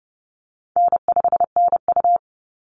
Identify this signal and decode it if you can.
Morse code, heard as a keyed tone, about 30 wpm, 700 Hz. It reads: N5NU
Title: Easy Call Signs Repeated 3x - 30wpm